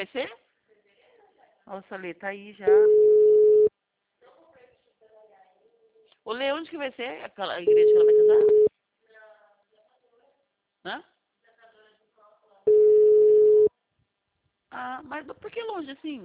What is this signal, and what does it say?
Ringback tone (Europe, 425 Hz), with speech.
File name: out- -2016- - - .0